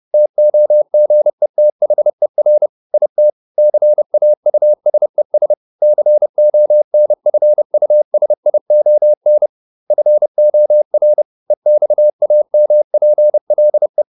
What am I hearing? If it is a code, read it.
Morse code, heard as a keyed tone, about 30 wpm, 600 Hz. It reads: TOGETHER IT CAUSES CONFUSION FOR EXAMPLE